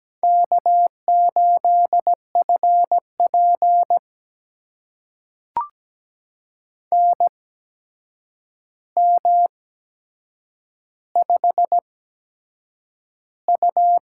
Morse code, heard as a keyed tone, about 17 wpm, spaced as for 7 wpm, 700 Hz. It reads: K8FP NM5U
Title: USA Call Signs - 4 Characters - 8x Character Spacing - 17wpm